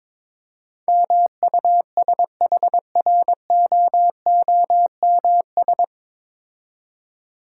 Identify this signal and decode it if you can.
Morse code, heard as a keyed tone, about 23 wpm, 690 Hz. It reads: MUSHROOMS
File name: Difficult Words with Hints - 22wpm